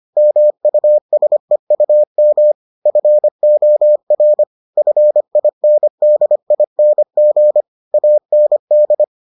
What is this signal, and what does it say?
Morse code, heard as a keyed tone, about 25 wpm, 600 Hz. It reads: MUSEUM FOR FINDING AND